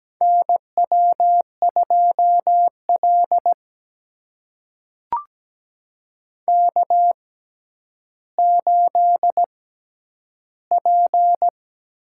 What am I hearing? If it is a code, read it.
Morse code, heard as a keyed tone, about 17 wpm, spaced as for 10 wpm, 700 Hz. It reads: NW2L K8P
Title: USA Call Signs - 4 Characters - 6x Character Spacing - 17wpm